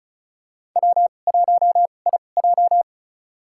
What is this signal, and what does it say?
Morse code, heard as a keyed tone, about 36 wpm, spaced as for 25 wpm, 700 Hz. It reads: W1IJ
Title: USA Call Signs - 4 Characters - 2x Character Spacing - 35wpm